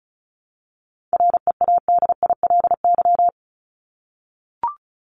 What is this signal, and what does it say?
Morse code, heard as a keyed tone, about 34 wpm, 700 Hz. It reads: READILY
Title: LY Suffix - Top 100 Words with Base Words 35wpm